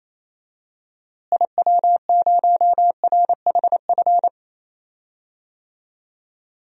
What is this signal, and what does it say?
Morse code, heard as a keyed tone, about 28 wpm, 700 Hz. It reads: IW0RHF
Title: DXCC-100 Call Signs - 6 Characters - Repeated 3x - 28wpm